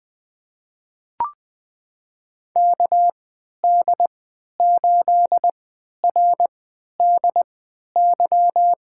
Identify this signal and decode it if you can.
Morse code, heard as a keyed tone, about 20 wpm, spaced as for 11 wpm, 700 Hz. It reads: KD8RDY
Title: USA Call Signs - 6 Characters - 3x Character Spacing - 20wpm